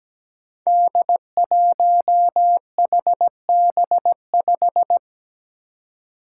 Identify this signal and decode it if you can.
Morse code, heard as a keyed tone, about 17 wpm, 700 Hz. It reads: D1HB5
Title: Five Challenging Letter-Number Combinations 17wpm